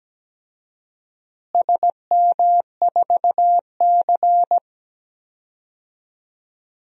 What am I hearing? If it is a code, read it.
Morse code, heard as a keyed tone, about 17 wpm, 700 Hz. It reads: SM4C
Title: DXCC-100 Call Signs - 4 Characters - Repeated 2x - 17wpm